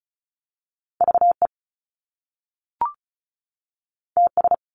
Morse code, heard as a keyed tone, about 36 wpm, 700 Hz. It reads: VE TS
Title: Top Two Letter Combinations in Words 35wpm